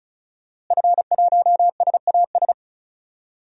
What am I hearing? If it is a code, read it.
Morse code, heard as a keyed tone, about 34 wpm, 700 Hz. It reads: F1SAS
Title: DXCC-100 Call Signs - 5 Characters - Repeated 3x - 35wpm